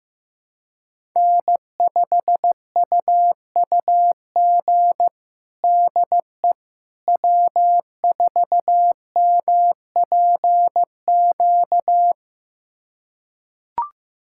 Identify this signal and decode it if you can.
Morse code, heard as a keyed tone, about 15 wpm, 700 Hz. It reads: N5UUG DE W4MPQ
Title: USA Callsign DE Callsign - 5 Characters - 15wpm